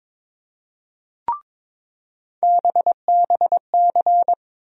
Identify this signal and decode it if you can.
Morse code, heard as a keyed tone, about 22 wpm, 700 Hz. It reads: BBC
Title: USA Call Sign Postfixes - 3 Letters 22wpm